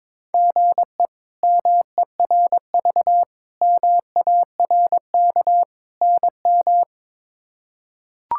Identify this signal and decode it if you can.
Morse code, heard as a keyed tone, about 22 wpm, 700 Hz. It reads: GE MERV MARK NM
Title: K1USN SST QSO Practice - 22wpm